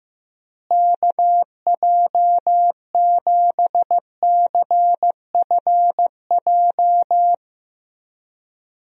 Morse code, heard as a keyed tone, about 15 wpm, 700 Hz. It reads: KJ7CFJ
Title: USA Call Signs - 6 Characters - Repeated 2x - 15wpm